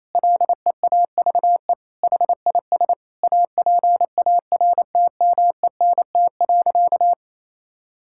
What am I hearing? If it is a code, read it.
Morse code, heard as a keyed tone, about 28 wpm, 700 Hz. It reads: LEAVE HIS APARTMENT.
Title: Sentences from Top 1000 Words - Review Part 6 - 28wpm